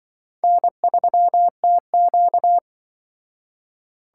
Morse code, heard as a keyed tone, about 23 wpm, 700 Hz. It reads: N3TQ
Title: Speed Racing - USA 1x2 Call Signs 20wpm